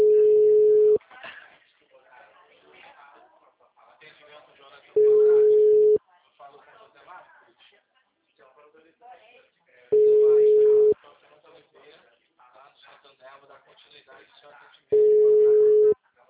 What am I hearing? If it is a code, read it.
Ringback tone (Europe, 425 Hz), with speech.
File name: out- -2040- - - .75177